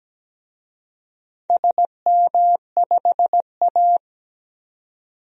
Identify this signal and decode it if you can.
Morse code, heard as a keyed tone, about 17 wpm, 700 Hz. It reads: SM5A
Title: DXCC-100 Call Signs - 4 Characters - 17wpm